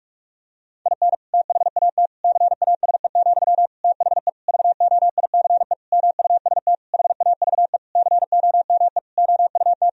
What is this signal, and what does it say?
Morse code, heard as a keyed tone, about 42 wpm, 700 Hz. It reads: IN THAT CASE, THE VOICE MUST HAVE COME OUT
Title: Sentences from Top 1000 Words - Review Part 2 - 45wpm